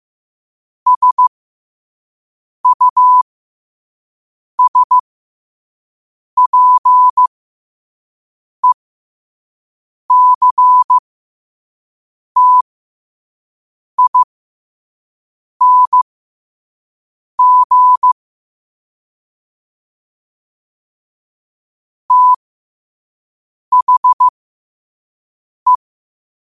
Morse code, heard as a keyed tone, about 15 wpm, spaced as for 5 wpm, 1000 Hz. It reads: SUSPECTING THE
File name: qotdfiles / 5wpm